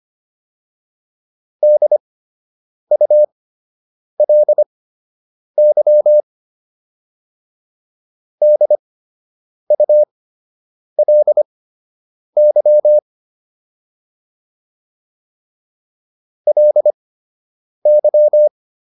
Morse code, heard as a keyed tone, about 25 wpm, spaced as for 8 wpm, 600 Hz. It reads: DULY DULY LY